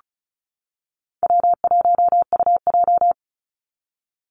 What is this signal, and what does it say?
Morse code, heard as a keyed tone, about 34 wpm, 700 Hz. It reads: W1UJ
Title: Call Signs - 35wpm